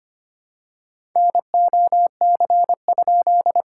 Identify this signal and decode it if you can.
Morse code, heard as a keyed tone, about 25 wpm, 700 Hz. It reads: NOC?
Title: SOTA QSO Practice 25wpm